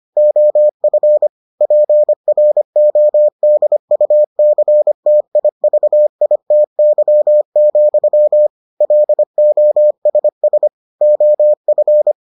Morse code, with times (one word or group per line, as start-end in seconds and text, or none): 0.2-1.3 OF
1.6-8.5 PRODUCTIVITY,
8.8-10.7 LOSS
11.0-12.1 OF